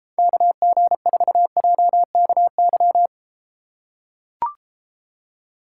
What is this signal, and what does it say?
Morse code, heard as a keyed tone, about 33 wpm, 700 Hz. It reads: KG4JKY